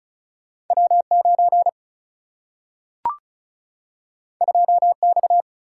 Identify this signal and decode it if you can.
Morse code, heard as a keyed tone, about 36 wpm, 700 Hz. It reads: W9 2X
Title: Single-Letter Single-Number 35wpm